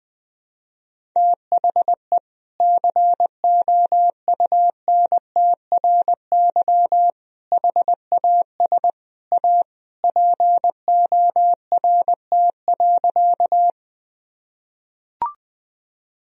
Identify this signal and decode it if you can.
Morse code, heard as a keyed tone, about 21 wpm, 700 Hz. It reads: THE COUNTRY HAS A PORT.
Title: Sentences from Top 1000 Words - Review Part 1 - 20wpm